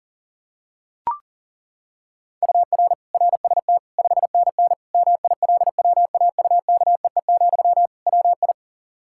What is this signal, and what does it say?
Morse code, heard as a keyed tone, about 40 wpm, 700 Hz. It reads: UR RST 5NN MILWAUKEE, WI
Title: K1USN SST QSO Practice 40wpm